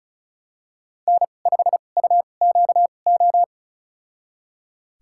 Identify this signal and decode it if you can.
Morse code, heard as a keyed tone, about 34 wpm, spaced as for 25 wpm, 700 Hz. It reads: N5UQO
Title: USA Call Signs - 5 Characters - 2x Character Spacing - 35wpm